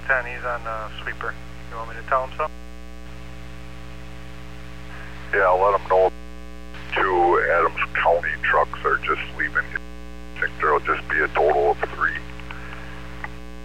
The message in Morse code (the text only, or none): none